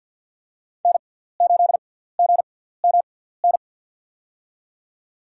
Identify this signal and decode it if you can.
Morse code, heard as a keyed tone, about 49 wpm, spaced as for 17 wpm, 700 Hz. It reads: N8GMN
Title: USA Call Signs - 5 Characters - 6x Character Spacing - 50wpm